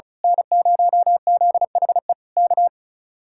Morse code, heard as a keyed tone, about 33 wpm, 700 Hz. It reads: N0ZHE K
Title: Speed-Racing Common QSO Phrases - 35wpm